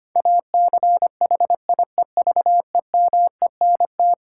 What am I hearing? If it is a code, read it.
Morse code, heard as a keyed tone, about 25 wpm, 700 Hz. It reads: ACHIEVEMENT
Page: morse-code-ninja-practice-25wpm